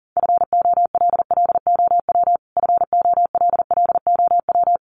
Morse code, heard as a keyed tone, about 40 wpm, 700 Hz. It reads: FOLLOW FOLLOW
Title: Top 200 Words Repeated 40wpm